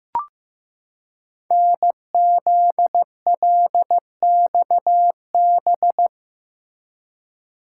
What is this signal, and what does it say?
Morse code, heard as a keyed tone, about 15 wpm, 700 Hz. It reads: NZLXB
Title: Code Groups - 5 Letters - 15wpm